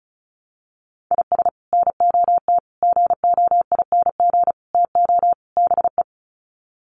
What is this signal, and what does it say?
Morse code, heard as a keyed tone, about 34 wpm, 700 Hz. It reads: IS NOT GOING TO BE
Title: Sets of 5 Words Encore2 35wpm